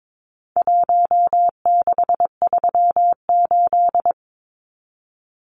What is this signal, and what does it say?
Morse code, heard as a keyed tone, about 22 wpm, 700 Hz. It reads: 1638
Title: Numbers - Contest Style - 4-Digit 22wpm